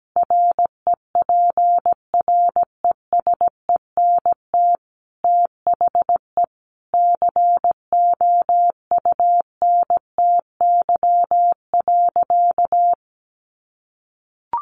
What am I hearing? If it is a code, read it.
Morse code, heard as a keyed tone, about 17 wpm, 700 Hz. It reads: REPRESENT THE COUNTY.
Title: Sentences from Top 1200 Words 17wpm